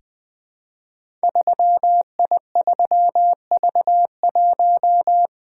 Morse code, heard as a keyed tone, about 20 wpm, 700 Hz. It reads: 3I3V1
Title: Code Groups - 5 Letter:Numbers - 20wpm